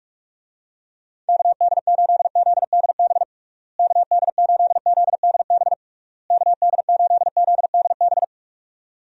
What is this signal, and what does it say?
Morse code, heard as a keyed tone, about 46 wpm, 700 Hz. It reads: KD8ZDB KD8ZDB KD8ZDB
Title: USA Call Signs - 6 Characters - Repeated 3x - 45wpm